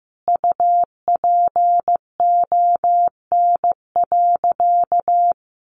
Morse code, heard as a keyed tone, about 15 wpm, 700 Hz. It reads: UPON.
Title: Sentences from Top 1100 Words 15wpm